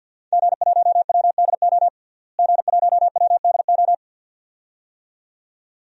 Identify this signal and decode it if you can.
Morse code, heard as a keyed tone, about 47 wpm, 700 Hz. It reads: G1WDO G1WDO
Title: DXCC-100 Call Signs - 5 Characters - Repeated 2x - 50wpm